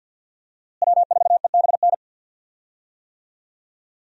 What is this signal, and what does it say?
Morse code, heard as a keyed tone, about 48 wpm, 700 Hz. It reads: W4EBN
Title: USA Call Signs - 5 Characters - Repeated 2x - 50wpm